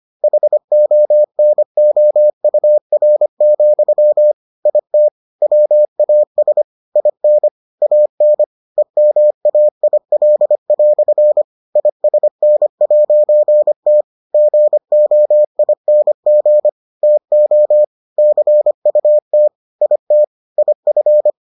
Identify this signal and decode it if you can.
Morse code, heard as a keyed tone, about 25 wpm, 600 Hz. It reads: HONOUR, IT WAS IN AN EMAIL" ISN'T GOING TO CUT IT IF